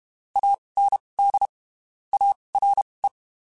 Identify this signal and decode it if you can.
Morse code, heard as a keyed tone, about 32 wpm, spaced as for 23 wpm, 800 Hz. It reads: AND ARE
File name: (spaced) / QSOs Spaced-mp3s / WA7QOX 1 @30 sp